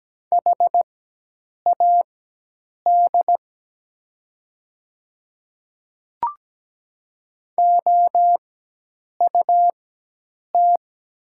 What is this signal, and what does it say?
Morse code, heard as a keyed tone, about 17 wpm, spaced as for 8 wpm, 700 Hz. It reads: HAD OUT
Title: 4x Character Spacing - Top 100 Words - 17wpm